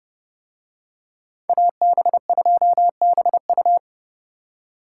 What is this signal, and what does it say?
Morse code, heard as a keyed tone, about 28 wpm, 700 Hz. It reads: AB2BU